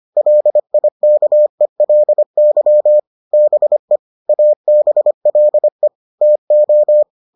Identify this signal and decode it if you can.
Morse code, heard as a keyed tone, about 25 wpm, 600 Hz. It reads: LIKELY BE ABLE TO